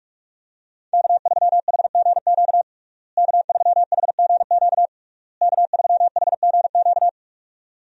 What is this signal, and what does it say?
Morse code, heard as a keyed tone, about 41 wpm, 700 Hz. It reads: K3HGQ K3HGQ K3HGQ
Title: USA Call Signs - 5 Characters - Repeated 3x - 45wpm